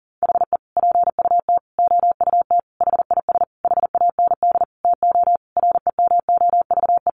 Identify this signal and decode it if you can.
Morse code, heard as a keyed tone, about 40 wpm, 700 Hz. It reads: HE PUT OUT HIS HAND TO REMOVE
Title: Sentences from Top 800 Words 40wpm